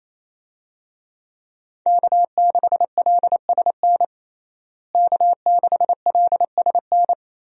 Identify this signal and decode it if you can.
Morse code, heard as a keyed tone, about 28 wpm, 700 Hz. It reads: K6LSN K6LSN
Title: USA Call Signs - 5 Characters - Repeated 2x - 28wpm